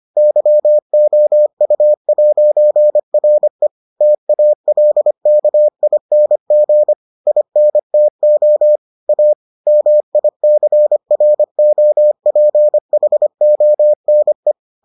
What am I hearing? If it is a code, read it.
Morse code, heard as a keyed tone, about 25 wpm, 600 Hz. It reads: YOU'RE TALKING INTO A MICROPHONE